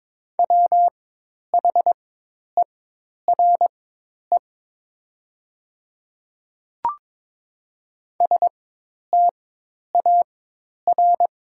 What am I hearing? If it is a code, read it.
Morse code, heard as a keyed tone, about 22 wpm, spaced as for 10 wpm, 700 Hz. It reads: WHERE STAR
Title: 4x Character Spacing - Top 500 Words - 22wpm